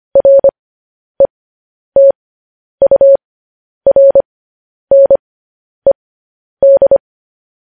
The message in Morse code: RETURNED

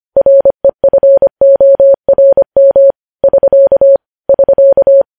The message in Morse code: REFORM <SK> <SK>

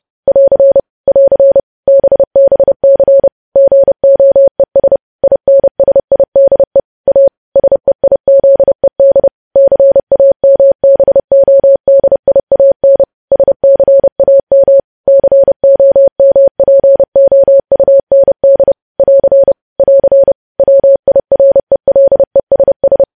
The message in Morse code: <AR> <AR> BBC GOES INSIDE A SEIZED CAMBODIAN SCAM COMPOUND <AR> <AR> WIRELESS

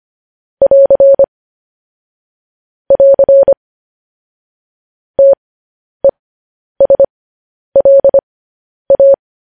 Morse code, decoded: <AR> <AR> TESLA

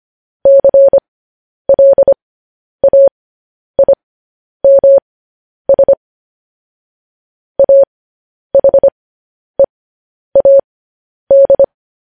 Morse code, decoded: CLAIMS AHEAD

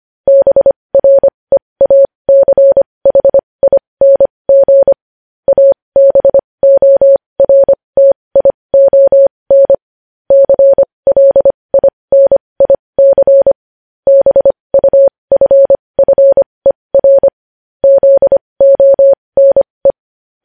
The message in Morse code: BREACHING ABORTION CLINIC BUFFER ZONE

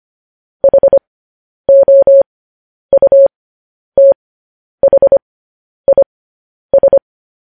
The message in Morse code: HOUTHIS

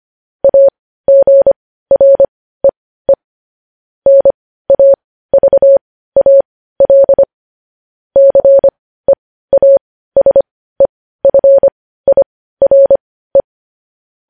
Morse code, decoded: AGREE NAVAL CEASEFIRE

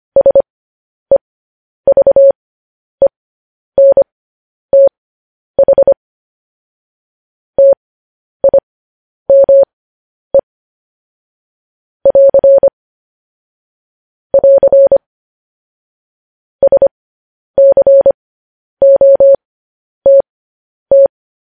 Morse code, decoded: SEVENTH TIME <AR> <AR> SCOTT

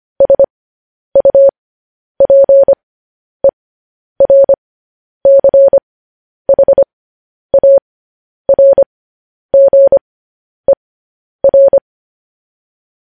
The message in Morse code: SUPERCHARGER

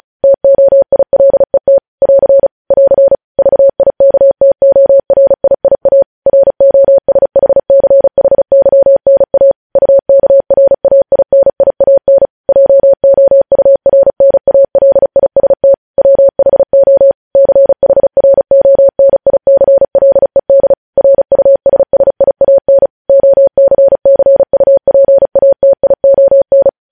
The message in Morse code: TOILET <AR> <AR> VIKTORIIA ROSHCHYNA UKRAINIAN JOURNALIST WHO CHRONICLED RUSSIAN OCCUPATION

